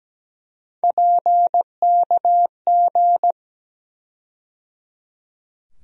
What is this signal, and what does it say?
Morse code, heard as a keyed tone, about 17 wpm, 700 Hz. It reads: PKG
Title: USA Call Sign Postfixes - 3 Letters 17wpm